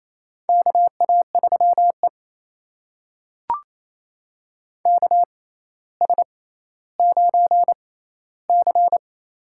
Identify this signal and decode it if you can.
Morse code, heard as a keyed tone, about 28 wpm, spaced as for 14 wpm, 700 Hz. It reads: KA3E KS9C
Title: USA Call Signs - 4 Characters - 6x Character Spacing - 28wpm